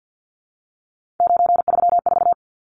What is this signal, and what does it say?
Morse code, heard as a keyed tone, about 50 wpm, 700 Hz. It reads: 934
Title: Numbers - 3-Digit 50wpm